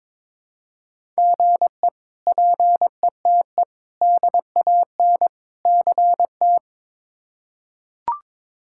Morse code, heard as a keyed tone, about 22 wpm, 700 Hz. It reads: GE PETE DAN CT
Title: K1USN SST QSO Practice - 22wpm